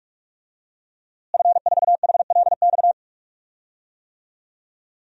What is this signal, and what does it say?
Morse code, heard as a keyed tone, about 46 wpm, 700 Hz. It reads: F4HLX
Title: DXCC-100 Call Signs - 5 Characters - Repeated 2x - 45wpm